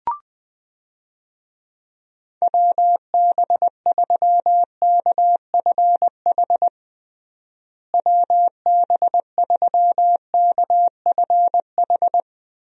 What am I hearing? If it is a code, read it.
Morse code, heard as a keyed tone, about 20 wpm, 700 Hz. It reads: WB3KFH WB3KFH
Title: USA Call Signs - 6 Characters - Repeated 2x - 20wpm